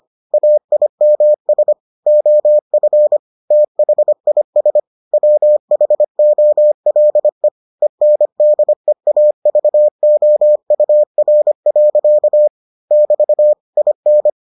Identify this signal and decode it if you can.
Morse code, heard as a keyed tone, about 25 wpm, 600 Hz. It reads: AIMS OF THIS WHOLE ENDEAVOUR. <BT> IN